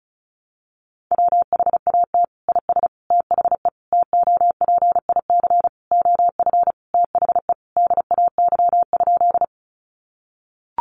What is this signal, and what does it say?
Morse code, heard as a keyed tone, about 34 wpm, 700 Hz. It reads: WHAT IS THE TOPIC OF THE DAY?
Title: Sentences from Top 1200 Words 35wpm